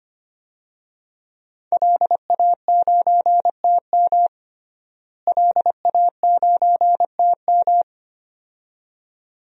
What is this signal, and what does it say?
Morse code, heard as a keyed tone, about 25 wpm, 700 Hz. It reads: LA9TM LA9TM
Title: DXCC-100 Call Signs - 5 Characters - Repeated 2x - 25wpm